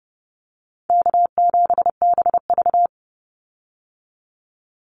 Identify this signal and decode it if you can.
Morse code, heard as a keyed tone, about 30 wpm, 700 Hz. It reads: K7BV
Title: Call Signs Repeated 2x - 30wpm